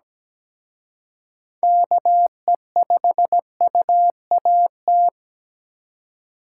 Morse code, heard as a keyed tone, about 20 wpm, 700 Hz. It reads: KE5UAT
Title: Speed Racing - USA 2x3 Call Signs 17wpm